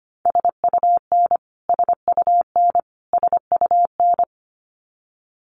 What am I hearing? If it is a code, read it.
Morse code, heard as a keyed tone, about 25 wpm, 700 Hz. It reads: SUN SUN SUN
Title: Top 500 Words Repeated 25wpm